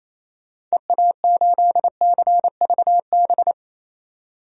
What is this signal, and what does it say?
Morse code, heard as a keyed tone, about 28 wpm, 700 Hz. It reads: EA8CVB
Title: DXCC-100 Call Signs - 6 Characters - 28wpm